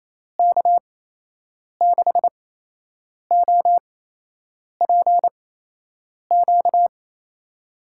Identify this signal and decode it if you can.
Morse code, heard as a keyed tone, about 28 wpm, spaced as for 8 wpm, 700 Hz. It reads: K6OPQ
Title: USA Call Signs - 6 Characters - 8x Character Spacing - 28wpm